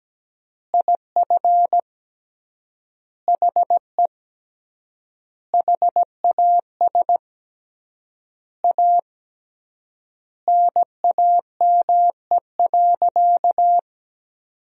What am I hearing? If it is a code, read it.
Morse code, heard as a keyed tone, about 17 wpm, 700 Hz. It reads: IF HE HAS A NAME.
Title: 3x Word-Spacing - Sentences from Top 500 Words - 17wpm